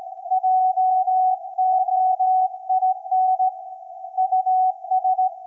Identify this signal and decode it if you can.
Morse code, heard as a keyed tone, about 17 wpm, 750 Hz. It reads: JOIN US
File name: iz7slz / VLF / SAQ 24-12-2024